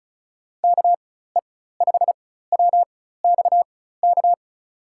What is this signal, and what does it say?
Morse code, heard as a keyed tone, about 34 wpm, spaced as for 16 wpm, 700 Hz. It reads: KE5WXK
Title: USA Call Signs - 4x Character Spacing - 35wpm